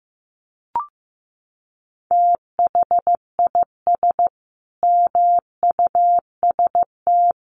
Morse code, heard as a keyed tone, about 15 wpm, 700 Hz. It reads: THIS MUST
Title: Sentences from Top 1100 Words 15wpm